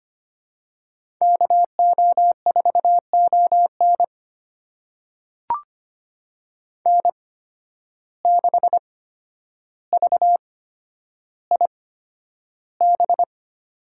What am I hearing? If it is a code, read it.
Morse code, heard as a keyed tone, about 25 wpm, spaced as for 11 wpm, 700 Hz. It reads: KO4ON N6VIB
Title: USA Call Signs - 5 Characters - 8x Character Spacing - 25wpm